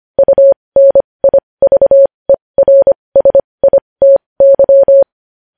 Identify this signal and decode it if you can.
Morse code, heard as a keyed tone, about 25 wpm, spaced as for 20 wpm, 550 Hz. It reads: UNIVERSITY
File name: podcast / News headlines-1x-25x20